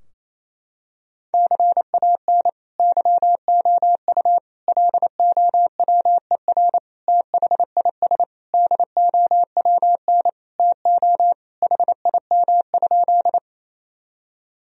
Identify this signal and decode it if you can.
Morse code, heard as a keyed tone, about 28 wpm, 700 Hz. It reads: CAN YOU LOWER THIS DOWN TO HIM?
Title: Sentences from Top 1000 Words - Review Part 4 - 28wpm